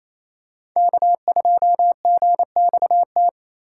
Morse code, heard as a keyed tone, about 28 wpm, 700 Hz. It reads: K2GXT
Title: USA Call Signs - 6x Character Spacing - 28wpm